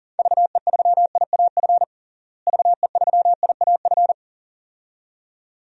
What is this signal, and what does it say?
Morse code, heard as a keyed tone, about 40 wpm, 700 Hz. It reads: VE3IAF VE3IAF
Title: DXCC-100 Call Signs - 6 Characters - Repeated 2x - 40wpm